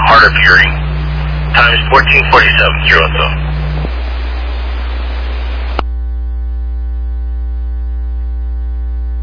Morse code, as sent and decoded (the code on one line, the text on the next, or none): none